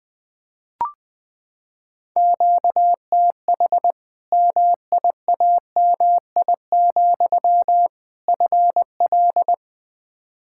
Morse code, heard as a keyed tone, about 20 wpm, 700 Hz. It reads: QTH MIAMI, FL
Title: Practice QSO - Exchange 1 - Beginning - 20wpm